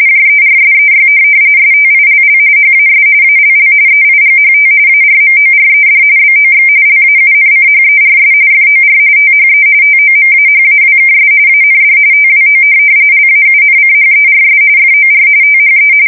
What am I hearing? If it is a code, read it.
RTTY, about 45.45 baud. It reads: JERSEY STILL HAS ALL EIGHT ARMORED BOX LAUNCHERS FOR THE TOMAHAWKS AND TWO LAUNCHERS FOR THE
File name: fsk170 / USS-New-Jersey